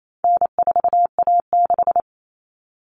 Morse code, heard as a keyed tone, about 28 wpm, 700 Hz. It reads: N4A6
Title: All Cut Numbers - 4-Digit 28wpm